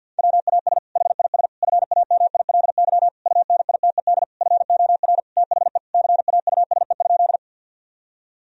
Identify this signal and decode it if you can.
Morse code, heard as a keyed tone, about 50 wpm, 700 Hz. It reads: WAS HIS FAMILY UNITED FOR THE CAUSE?
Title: Sentences from Top 1000 Words - Review Part 4 - 50wpm